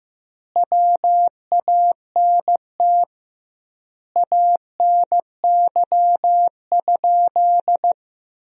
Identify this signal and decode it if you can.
Morse code, heard as a keyed tone, about 15 wpm, 700 Hz. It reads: WANT ANY?
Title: 2x Word-Spacing - Sentences from Top 300 Words - 15wpm